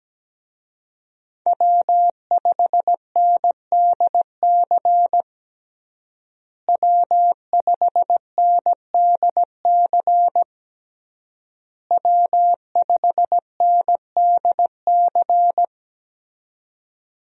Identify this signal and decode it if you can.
Morse code, heard as a keyed tone, about 17 wpm, 700 Hz. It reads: W5NDC W5NDC W5NDC
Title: USA Call Signs - 5 Characters - Repeated 3x - 17wpm